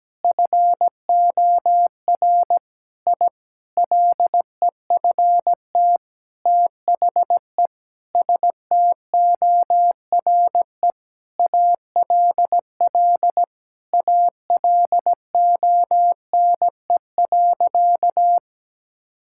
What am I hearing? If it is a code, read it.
Morse code, heard as a keyed tone, about 17 wpm, 700 Hz. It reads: FOR I LEFT THE STORE ALL ALONE.
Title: Sentences from Top 1000 Words - Review Part 5 - 17wpm